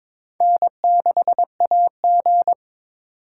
Morse code, heard as a keyed tone, about 22 wpm, 700 Hz. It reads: N6AG
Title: USA Call Signs - 6x Character Spacing - 22wpm